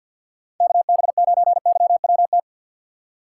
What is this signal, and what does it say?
Morse code, heard as a keyed tone, about 46 wpm, 700 Hz. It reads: KB9YWT
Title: USA Call Signs - 6 Characters - 6x Character Spacing - 50wpm